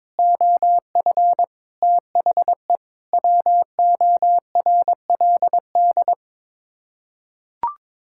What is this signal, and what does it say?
Morse code, heard as a keyed tone, about 22 wpm, 700 Hz. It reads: OF THE WORLD
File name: practice / 22wpm / Sets of 6 Words - 22wpm